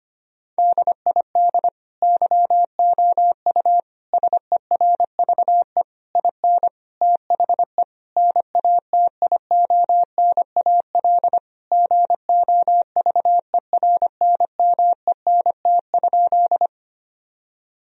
Morse code, heard as a keyed tone, about 25 wpm, 700 Hz. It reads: DID YOU SERVE IN THE NATIONAL GOVERNMENT?
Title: Sentences from Top 1000 Words - Review Part 4 - 25wpm